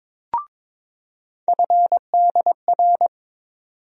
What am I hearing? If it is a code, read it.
Morse code, heard as a keyed tone, about 22 wpm, 700 Hz. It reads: FDR
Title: USA Call Sign Postfixes - 3 Letters 22wpm